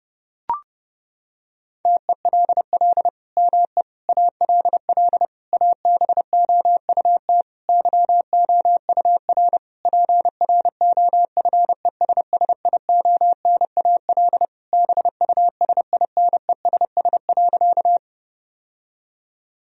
Morse code, heard as a keyed tone, about 30 wpm, 700 Hz. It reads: TELL ME ALL ABOUT YOUR PROFESSIONAL BUSINESS.
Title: Sentences from Top 1000 Words - Review Part 4 - 30wpm